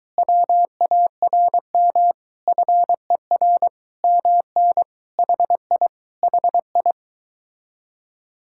Morse code, heard as a keyed tone, about 24 wpm, 700 Hz. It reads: WARM FER MN HI HI